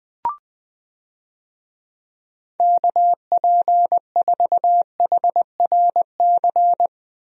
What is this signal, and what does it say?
Morse code, heard as a keyed tone, about 20 wpm, 700 Hz. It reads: KP4HRC
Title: USA Call Signs - 6 Characters - Repeated 2x - 20wpm